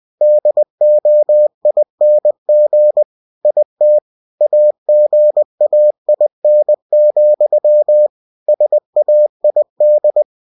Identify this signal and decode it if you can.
Morse code, heard as a keyed tone, about 20 wpm, 600 Hz. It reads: DOING IT AGAIN, SAID
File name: cw-books / winnie-the-pooh / winnie-20wpm-page-0030